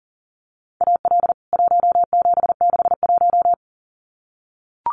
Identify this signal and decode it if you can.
Morse code, heard as a keyed tone, about 40 wpm, 700 Hz. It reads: AL 1761
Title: CWT Exchange 40wpm